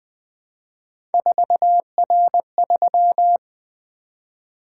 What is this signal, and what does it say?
Morse code, heard as a keyed tone, about 20 wpm, 700 Hz. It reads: 4R3
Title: Three Letter-Number Combinations - 20wpm